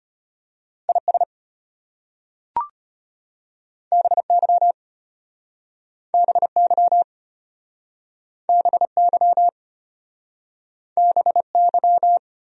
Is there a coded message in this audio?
Yes: Morse code from 0.9 to 12.2 s, heard as a keyed tone, about 33 wpm, 700 Hz. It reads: IS BY BY BY BY